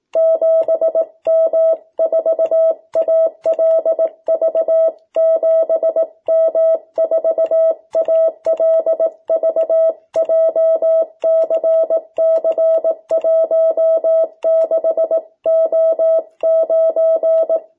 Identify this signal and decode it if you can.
Morse code, heard as a keyed tone, about 18 wpm, 660 Hz. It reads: 7M4ALV 7M4ALV JCC16O9
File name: jh8cbh / 2021cw